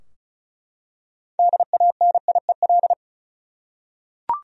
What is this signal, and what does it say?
Morse code, heard as a keyed tone, about 36 wpm, 700 Hz. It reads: DANIEL
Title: USA Common Names - 35wpm